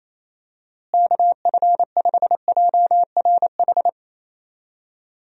Morse code, heard as a keyed tone, about 28 wpm, 700 Hz. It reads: KF5JRH
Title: USA Call Signs - 6 Characters - Repeated 2x - 28wpm